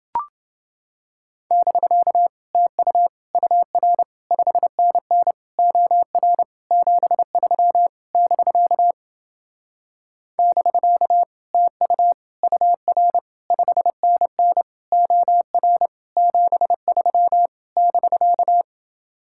Morse code, heard as a keyed tone, about 28 wpm, 700 Hz. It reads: <BK> TU UR 5NN OR 73 <BK> <BK> TU UR 5NN OR 73 <BK>